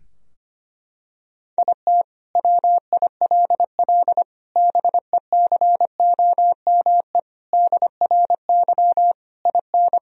Morse code, heard as a keyed tone, about 25 wpm, 700 Hz. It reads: IT WILL BECOME DRY IN
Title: Sentences from Top 1000 Words - Review Part 3 - 25wpm